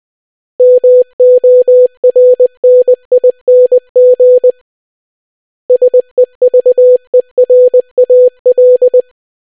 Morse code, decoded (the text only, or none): MORNING SEVERAL